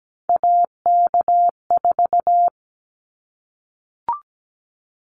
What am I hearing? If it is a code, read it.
Morse code, heard as a keyed tone, about 17 wpm, 700 Hz. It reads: AK4
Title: Callsign Prefixes for Top 25 DXCC - 17wpm